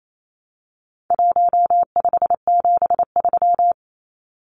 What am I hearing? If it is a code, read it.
Morse code, heard as a keyed tone, about 28 wpm, 700 Hz. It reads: 1573
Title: Numbers - Contest Style - 4-Digit 28wpm